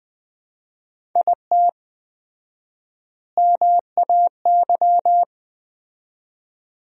Morse code, heard as a keyed tone, about 20 wpm, 700 Hz. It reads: IT MAY
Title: 4x Word-Spacing - Sentences from Top 400 Words - 20wpm